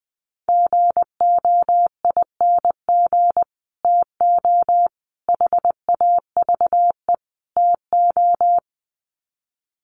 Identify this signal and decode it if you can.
Morse code, heard as a keyed tone, about 20 wpm, 700 Hz. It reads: GOING TO HAVE TO
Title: Sets of 4 Words 20wpm